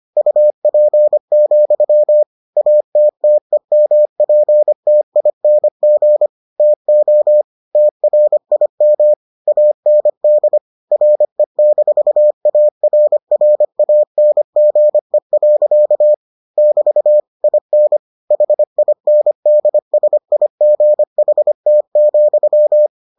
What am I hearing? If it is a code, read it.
Morse code, heard as a keyed tone, about 25 wpm, 600 Hz. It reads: UP, ATTEMPTING TO TRIM AND RE-ARRANGE. <BT> IN HINDSIGHT,